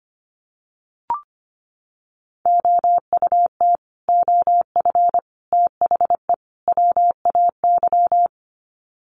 Morse code, heard as a keyed tone, about 25 wpm, 700 Hz. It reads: OUT OF THE WAY
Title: Sets of 4 Words 25wpm